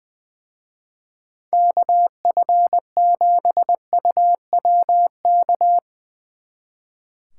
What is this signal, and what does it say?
Morse code, heard as a keyed tone, about 22 wpm, 700 Hz. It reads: KF7UWK
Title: Speed Racing - USA 2x3 Call Signs 20wpm